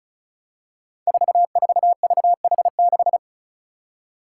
Morse code, heard as a keyed tone, about 36 wpm, 700 Hz. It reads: 44VH6
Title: Five Challenging Letter-Number Combinations 35wpm